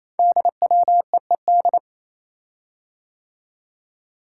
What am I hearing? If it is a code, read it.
Morse code, heard as a keyed tone, about 27 wpm, 700 Hz. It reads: DWEED
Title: Code Groups - 5 Letter:Numbers - 28wpm